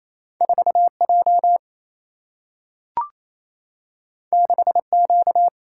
Morse code, heard as a keyed tone, about 28 wpm, 700 Hz. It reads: 4 J 6 Q
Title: Single-Letter Single-Number 28wpm